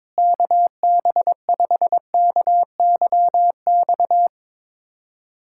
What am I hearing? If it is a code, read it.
Morse code, heard as a keyed tone, about 22 wpm, 700 Hz. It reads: KB5KYX
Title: K1USN SST QSO Practice - 22wpm